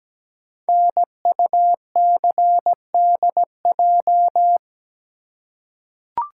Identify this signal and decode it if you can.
Morse code, heard as a keyed tone, about 17 wpm, 700 Hz. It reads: NUCDJ
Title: Code Groups - 5 Letters - 17wpm